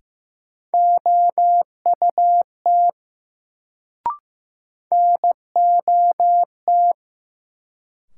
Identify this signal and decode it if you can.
Morse code, heard as a keyed tone, about 15 wpm, 700 Hz. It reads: OUT NOT
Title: Sets of 2 Words - New Word Drill - 15wpm